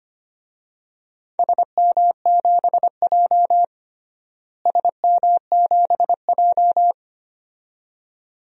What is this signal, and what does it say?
Morse code, heard as a keyed tone, about 25 wpm, 700 Hz. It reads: SM7J SM7J
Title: DXCC-100 Call Signs - 4 Characters - Repeated 2x - 25wpm